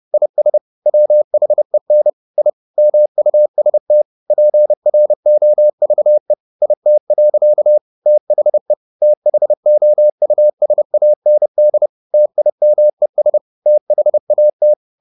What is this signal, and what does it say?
Morse code, heard as a keyed tone, about 30 wpm, 600 Hz. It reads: IS WHEN I MUST PROVE IT. THE THOUSAND TIMES THAT
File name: the old man and the sea / the old man and the sea 1 30wpm